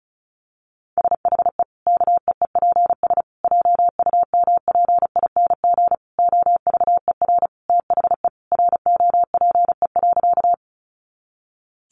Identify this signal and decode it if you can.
Morse code, heard as a keyed tone, about 34 wpm, 700 Hz. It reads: SHE KEEPS JUMPING OVER THE ROPE.
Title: Easy Phrases 35wpm